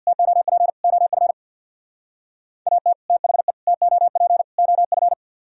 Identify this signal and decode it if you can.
Morse code, heard as a keyed tone, about 64 wpm, 700 Hz. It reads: TOP OF AT THE TOP OF